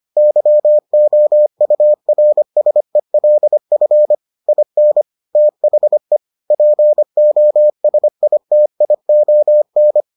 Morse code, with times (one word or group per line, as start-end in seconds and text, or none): 0.2-4.1 YOURSELF
4.5-5.0 IN
5.4-6.2 THE
6.5-10.0 POSITION